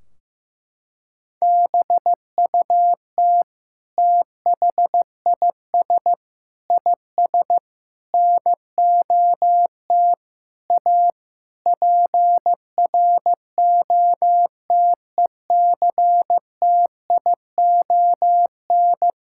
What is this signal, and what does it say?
Morse code, heard as a keyed tone, about 15 wpm, 700 Hz. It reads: BUT THIS IS NOT A PROTECTION
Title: Sentences from Top 1600 Words 15wpm